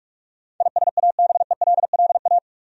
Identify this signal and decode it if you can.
Morse code, heard as a keyed tone, about 41 wpm, 700 Hz. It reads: ISABELLA
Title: USA Common Names - 45wpm